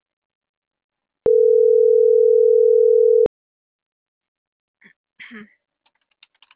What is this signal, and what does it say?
Ringback tone (North America), 440 + 480 Hz.